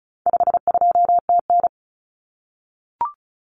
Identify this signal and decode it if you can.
Morse code, heard as a keyed tone, about 34 wpm, 700 Hz. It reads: 52TN
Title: Common Cut Numbers - 4-Digit 35wpm